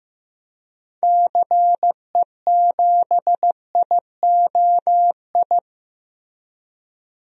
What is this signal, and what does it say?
Morse code, heard as a keyed tone, about 15 wpm, 700 Hz. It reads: CE7IOI
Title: DXCC-100 Call Signs - 6 Characters - Repeated 3x - 15wpm